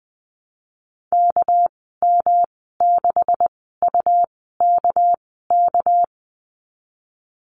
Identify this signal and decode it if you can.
Morse code, heard as a keyed tone, about 19 wpm, spaced as for 14 wpm, 700 Hz. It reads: KM6UKK
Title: 200 Lesson 32g - Letter k - Call Signs 20wpm